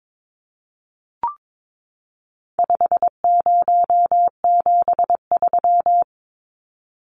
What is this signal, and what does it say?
Morse code, heard as a keyed tone, about 22 wpm, 700 Hz. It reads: 5073
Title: Numbers - 4-Digit 22wpm